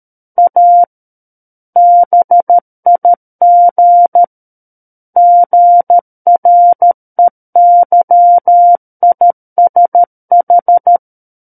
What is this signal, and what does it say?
Morse code, heard as a keyed tone, about 13 wpm, 700 Hz. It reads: A BIG GREYISH
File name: Book One / 04-Chapter 4 - 13 WPM